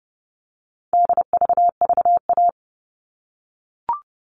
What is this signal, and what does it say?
Morse code, heard as a keyed tone, about 30 wpm, 700 Hz. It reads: DVVA
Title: All Cut Numbers - 4-Digit 30wpm